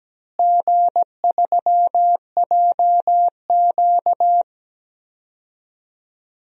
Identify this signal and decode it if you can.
Morse code, heard as a keyed tone, about 17 wpm, 700 Hz. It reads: G3JQ
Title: DXCC-100 Call Signs - 4 Characters - Repeated 3x - 17wpm